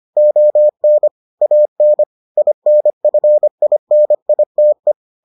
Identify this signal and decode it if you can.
Morse code, heard as a keyed tone, about 25 wpm, 600 Hz. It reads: ON AN INFINITE